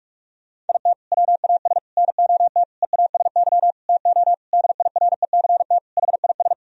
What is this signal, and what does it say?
Morse code, heard as a keyed tone, about 46 wpm, 700 Hz. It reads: IT WAS NOT EASY TO DIRECT HIS
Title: Sentences from Top 1000 Words - Review Part 2 - 45wpm